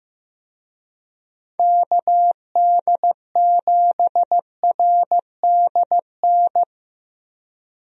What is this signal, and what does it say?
Morse code, heard as a keyed tone, about 15 wpm, 700 Hz. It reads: KD7RDN
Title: USA Call Signs - 6 Characters - Repeated 3x - 15wpm